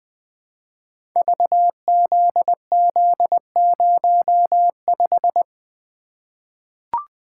Morse code, heard as a keyed tone, about 20 wpm, 700 Hz. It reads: VZZ05
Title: Code Groups - 5 Letter:Numbers - 20wpm